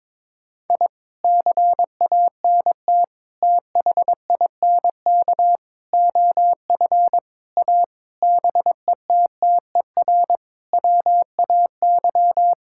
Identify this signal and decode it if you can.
Morse code, heard as a keyed tone, about 22 wpm, 700 Hz. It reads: I CANT THINK OF A BETTER WAY
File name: practice / 22wpm / Sets of 8 Words - 22wpm